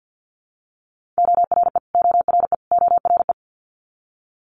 Morse code, heard as a keyed tone, about 50 wpm, 690 Hz. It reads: ORE ORE ORE
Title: Multi-letter Phonemes Repeated 3x 50wpm